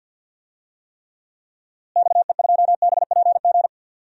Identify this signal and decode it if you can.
Morse code, heard as a keyed tone, about 51 wpm, 700 Hz. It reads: XE2BPG
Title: DXCC-100 Call Signs - 6 Characters - Repeated 3x - 50wpm